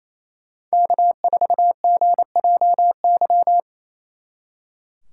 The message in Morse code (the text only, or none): K4GJY